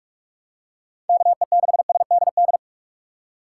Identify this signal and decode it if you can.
Morse code, heard as a keyed tone, about 43 wpm, 700 Hz. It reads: KE6SDD
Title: USA Call Signs - 6 Characters - 4x Character Spacing - 45wpm